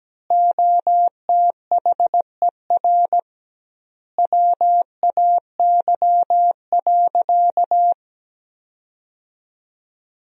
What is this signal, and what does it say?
Morse code, heard as a keyed tone, about 17 wpm, 700 Hz. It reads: OTHER WAY.
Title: 2x Word-Spacing - Sentences from Top 100 Words - 17wpm